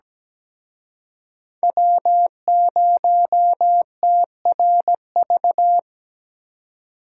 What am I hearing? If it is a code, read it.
Morse code, heard as a keyed tone, about 20 wpm, 700 Hz. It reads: W0TRV
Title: Speed Racing - USA 1x3 Call Signs 17wpm